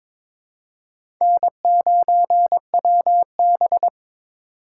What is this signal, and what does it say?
Morse code, heard as a keyed tone, about 21 wpm, 700 Hz. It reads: N9WB
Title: USA Call Signs - 4 Characters - 3x Character Spacing - 22wpm